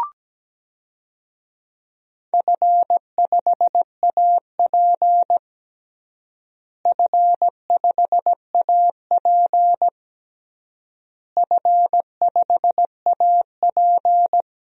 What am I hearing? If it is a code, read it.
Morse code, heard as a keyed tone, about 17 wpm, 700 Hz. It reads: F5AP F5AP F5AP
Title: DXCC-100 Call Signs - 4 Characters - Repeated 3x - 17wpm